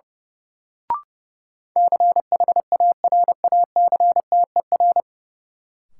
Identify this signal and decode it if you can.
Morse code, heard as a keyed tone, about 32 wpm, 700 Hz. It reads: CHARACTER
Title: Sentences from Top 1200 Words - New Word Drill Alt - 30wpm